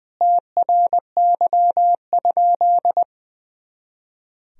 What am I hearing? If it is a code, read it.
Morse code, heard as a keyed tone, about 20 wpm, 700 Hz. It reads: TRY?
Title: Sentences from Top 200 Words and Abbreviations 20wpm